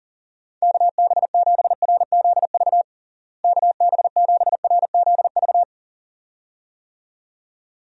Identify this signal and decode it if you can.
Morse code, heard as a keyed tone, about 40 wpm, 700 Hz. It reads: KB7RZV KB7RZV
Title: USA Call Signs - 6 Characters - Repeated 2x - 40wpm